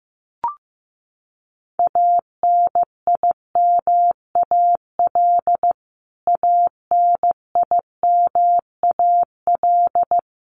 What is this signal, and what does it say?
Morse code, heard as a keyed tone, about 15 wpm, 700 Hz. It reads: ANIMAL ANIMAL
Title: Top 500 Words Repeated 15wpm